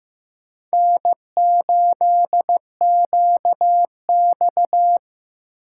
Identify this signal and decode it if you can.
Morse code, heard as a keyed tone, about 15 wpm, 700 Hz. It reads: N8QX
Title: USA Call Signs - 4 Characters - 8x Character Spacing - 15wpm